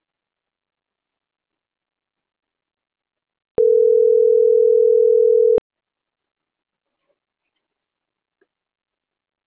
Ringback tone (North America), 440 + 480 Hz.